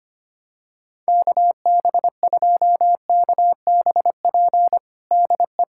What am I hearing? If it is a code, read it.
Morse code, heard as a keyed tone, about 25 wpm, 700 Hz. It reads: KB2KBP DE
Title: USA Callsign DE Callsign - 6 Characters - 25wpm